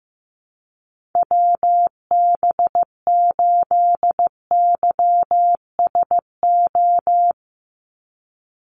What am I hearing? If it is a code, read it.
Morse code, heard as a keyed tone, about 15 wpm, 700 Hz. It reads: WB8YSO
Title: USA Call Signs 15wpm